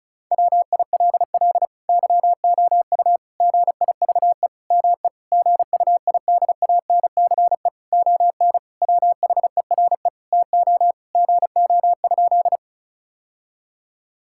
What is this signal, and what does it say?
Morse code, heard as a keyed tone, about 34 wpm, 700 Hz. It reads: WILL YOU GIVE ME GUIDANCE ON WHERE TO GO?
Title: Sentences from Top 1600 Words 35wpm